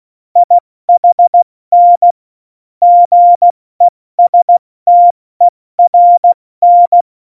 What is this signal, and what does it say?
Morse code, heard as a keyed tone, about 16 wpm, 700 Hz. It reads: IHN GESTERN
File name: Morsekurs / Witze / Witz 07 April 2020, 70 F80